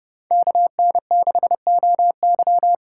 Morse code, heard as a keyed tone, about 32 wpm, 700 Hz. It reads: KN6OY